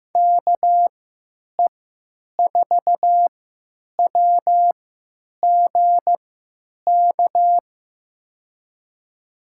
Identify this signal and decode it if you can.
Morse code, heard as a keyed tone, about 16 wpm, spaced as for 9 wpm, 700 Hz. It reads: KE4WGK
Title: USA Call Signs - 6 Characters - 3x Character Spacing - 15wpm